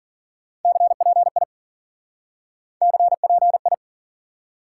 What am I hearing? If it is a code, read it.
Morse code, heard as a keyed tone, about 52 wpm, 700 Hz. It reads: CPI CPI